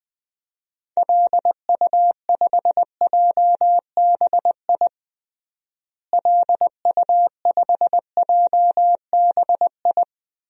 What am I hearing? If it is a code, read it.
Morse code, heard as a keyed tone, about 20 wpm, 700 Hz. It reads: LU5JBI LU5JBI
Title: DXCC-100 Call Signs - 6 Characters - Repeated 3x - 20wpm